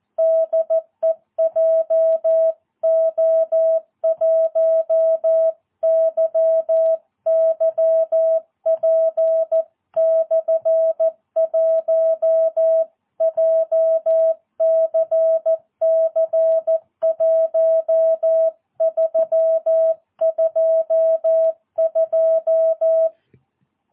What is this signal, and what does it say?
Morse code, heard as a keyed tone, about 14 wpm, 650 Hz. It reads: DE JO1YYP/1 JCC 1322
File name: jh8cbh / 2022cw